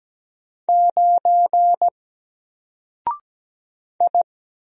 Morse code, heard as a keyed tone, about 17 wpm, 700 Hz. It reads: 9 I
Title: Single Character (Letter:Number:Punctuation:Prosign) 17wpm